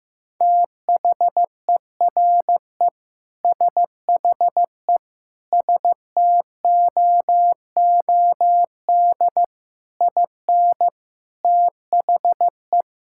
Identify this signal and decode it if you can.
Morse code, heard as a keyed tone, about 15 wpm, 700 Hz. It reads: THERE SHE STOOD IN THE
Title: Sentences from Top 1600 Words 15wpm